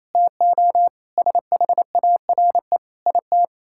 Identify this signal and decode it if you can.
Morse code, heard as a keyed tone, about 28 wpm, 700 Hz. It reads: TO SHARE IT
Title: Sentences from Top 1000 Words - Review Part 6 - 28wpm